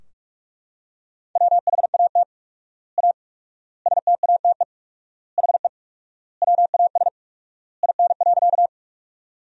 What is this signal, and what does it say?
Morse code, heard as a keyed tone, about 43 wpm, 700 Hz. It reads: WHAT A STATE HE WAS IN.
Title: 4x Word-Spacing - Sentences from Top 400 Words - 45wpm